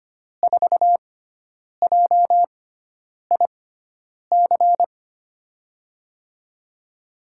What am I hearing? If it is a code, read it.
Morse code, heard as a keyed tone, about 25 wpm, spaced as for 9 wpm, 700 Hz. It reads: 4JIC
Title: USA Call Signs - 6 Characters - 6x Character Spacing - 25wpm